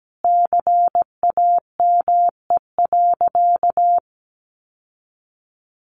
Morse code, heard as a keyed tone, about 17 wpm, 700 Hz. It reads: CAME.
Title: Sentences from Top 1400 Words 17wpm